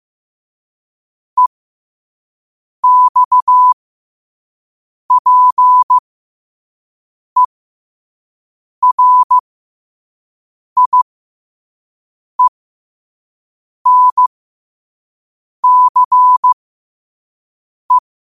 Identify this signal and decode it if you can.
Morse code, heard as a keyed tone, about 15 wpm, spaced as for 5 wpm, 1000 Hz. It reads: EXPERIENCE
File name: qotdfiles / 5wpm